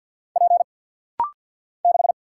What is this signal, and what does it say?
Morse code, heard as a keyed tone, about 48 wpm, 700 Hz. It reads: P 6
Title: Single Letter-Number - Rapid-Fire 50wpm